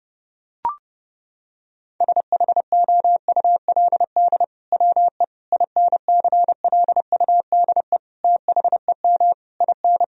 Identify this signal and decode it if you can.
Morse code, heard as a keyed tone, about 30 wpm, 700 Hz. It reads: SHOULD WE INCLUDE THEM IN
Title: Sentences from Top 1000 Words - Review Part 3 - 30wpm